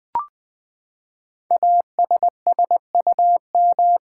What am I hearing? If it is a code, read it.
Morse code, heard as a keyed tone, about 20 wpm, 700 Hz. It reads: ASSUM
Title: Build A Word - 5 Letters At A Time - 10-Letter Words - 20wpm